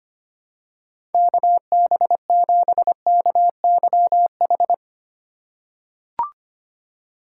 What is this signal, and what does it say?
Morse code, heard as a keyed tone, about 25 wpm, 700 Hz. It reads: KB7KYH
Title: USA Call Signs - 2x Character Spacing - 25wpm